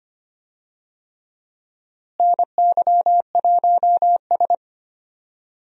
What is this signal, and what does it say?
Morse code, heard as a keyed tone, about 25 wpm, 700 Hz. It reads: NY1S
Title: USA Call Signs - 4 Characters - Repeated 3x - 25wpm